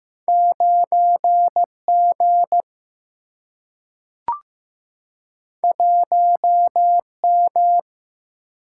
Morse code, heard as a keyed tone, about 15 wpm, 700 Hz. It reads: 9G 1M
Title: Single-Letter Single-Number 15wpm